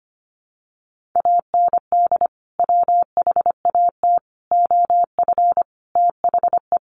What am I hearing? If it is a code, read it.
Morse code, heard as a keyed tone, about 25 wpm, 700 Hz. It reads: AND WHAT OF THE
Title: Sentences from Top 1500 Words 25wpm